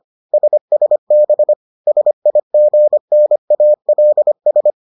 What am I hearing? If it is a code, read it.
Morse code, heard as a keyed tone, about 25 wpm, 600 Hz. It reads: SSB SIGNALS